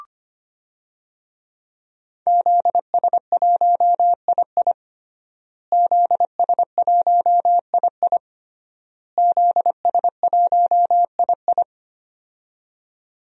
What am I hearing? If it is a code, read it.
Morse code, heard as a keyed tone, about 25 wpm, 700 Hz. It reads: ZS1II ZS1II ZS1II
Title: DXCC-100 Call Signs - 5 Characters - Repeated 3x - 25wpm